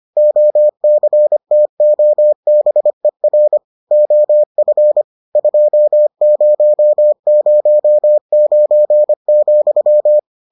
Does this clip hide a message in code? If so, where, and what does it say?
Yes: Morse code from 0.2 to 10.2 s, heard as a keyed tone, about 25 wpm, 600 Hz. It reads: OCTOBER OF 2009,